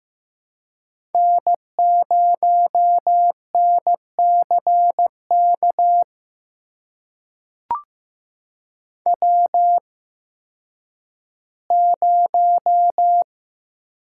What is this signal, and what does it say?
Morse code, heard as a keyed tone, about 15 wpm, spaced as for 10 wpm, 700 Hz. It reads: N0NCK W0
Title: USA Call Signs - 5 Characters - 8x Character Spacing - 15wpm